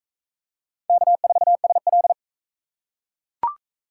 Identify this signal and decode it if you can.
Morse code, heard as a keyed tone, about 40 wpm, 700 Hz. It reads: K4SL